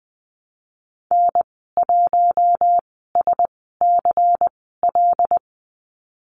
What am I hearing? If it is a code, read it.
Morse code, heard as a keyed tone, about 20 wpm, spaced as for 15 wpm, 700 Hz. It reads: N1SCL
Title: 128 Lesson 21g - Number 6 - Call Signs 20wpm